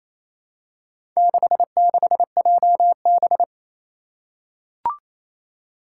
Morse code, heard as a keyed tone, about 27 wpm, 700 Hz. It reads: 66JB
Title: Four Challenging Letter-Number Combinations 28wpm